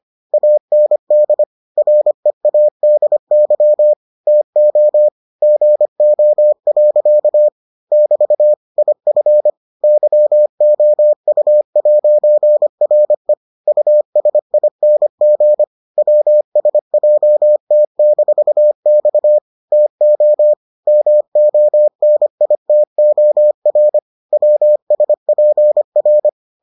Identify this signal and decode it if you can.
Morse code, heard as a keyed tone, about 25 wpm, 600 Hz. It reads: AND READY TO GO. <BT> IF YOU'RE USING WSJT-X TO MONITOR WSPR